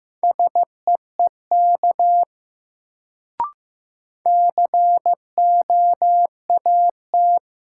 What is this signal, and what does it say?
Morse code, heard as a keyed tone, about 15 wpm, 700 Hz. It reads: SEEK COAT
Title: Sentences from Top 1600 Words - New Word Drill - 15wpm